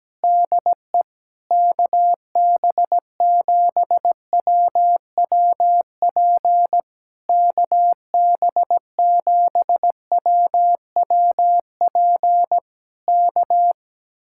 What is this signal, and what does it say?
Morse code, heard as a keyed tone, about 17 wpm, 700 Hz. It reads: DE KB7WWP KB7WWP K
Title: Speed-Racing Common QSO Phrases Encore - 17wpm